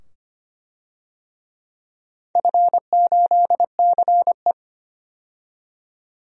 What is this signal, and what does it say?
Morse code, heard as a keyed tone, about 25 wpm, 700 Hz. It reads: F8CE
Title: DXCC-100 Call Signs - 4 Characters - 25wpm